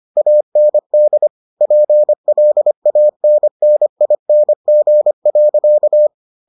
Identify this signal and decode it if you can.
Morse code, heard as a keyed tone, about 25 wpm, 600 Hz. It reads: AND PLANNING.